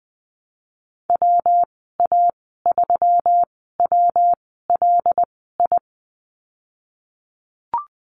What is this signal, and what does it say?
Morse code, heard as a keyed tone, about 20 wpm, spaced as for 14 wpm, 700 Hz. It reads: WA3WLI
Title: 140 Lesson 23g - Letter f - Call Signs 20wpm